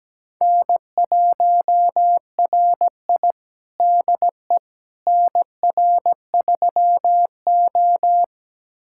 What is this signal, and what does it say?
Morse code, heard as a keyed tone, about 17 wpm, 700 Hz. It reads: N1RI DE NR3O
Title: USA Callsign DE Callsign - 4 Characters - 17wpm